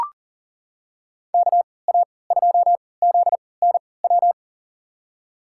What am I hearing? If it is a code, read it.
Morse code, heard as a keyed tone, about 40 wpm, spaced as for 23 wpm, 700 Hz. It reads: KA2ZNW
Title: USA Call Signs - 6 Characters - 3x Character Spacing - 40wpm